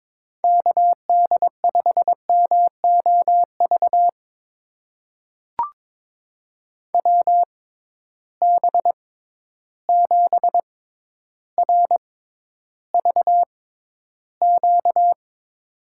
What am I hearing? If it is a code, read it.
Morse code, heard as a keyed tone, about 22 wpm, spaced as for 11 wpm, 700 Hz. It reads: KD5MOV WB7RVQ
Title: USA Call Signs - 6 Characters - 6x Character Spacing - 22wpm